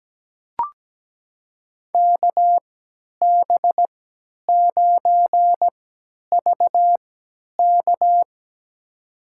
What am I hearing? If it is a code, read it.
Morse code, heard as a keyed tone, about 17 wpm, spaced as for 10 wpm, 700 Hz. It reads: KB9VK
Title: USA Call Signs - 5 Characters - 3x Character Spacing - 17wpm